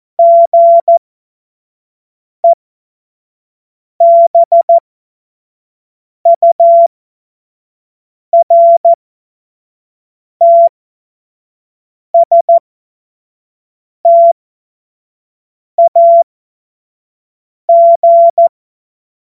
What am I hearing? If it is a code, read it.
Morse code, heard as a keyed tone, about 14 wpm, spaced as for 5 wpm, 680 Hz. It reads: GEBURTSTAG
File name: Morsekurs / Witze / Witz 15. August 2023 25 F70